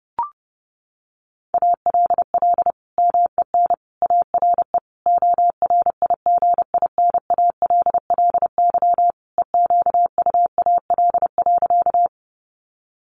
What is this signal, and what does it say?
Morse code, heard as a keyed tone, about 30 wpm, 700 Hz. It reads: ALL MEN ARE ORIGINALLY EQUAL.
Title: Sentences from Top 1500 Words 30wpm